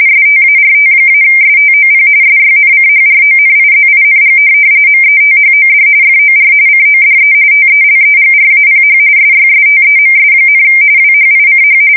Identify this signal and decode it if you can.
RTTY, about 45.45 baud. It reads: START-STOP SYNCHRONIZING METHOD FOR CODE TELEGRAPH SYSTEMS, WHICH
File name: fsk170 / teletype-corp